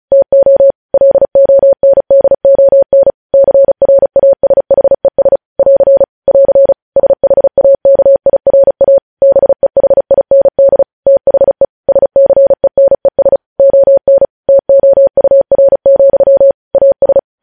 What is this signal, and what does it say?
Morse code, heard as a keyed tone, about 36 wpm, 550 Hz. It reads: TO LONDON CRASHES <AR> <AR> SHAKIRA BEHIND THE SCENES ON TOUR, AS
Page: news headlines in morse code at 35 wpm